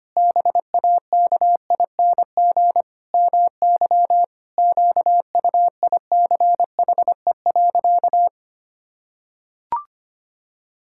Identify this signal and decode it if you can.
Morse code, heard as a keyed tone, about 25 wpm, 700 Hz. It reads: BAKING MY QUICHE.
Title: Pangram Challenge v2 - 25wpm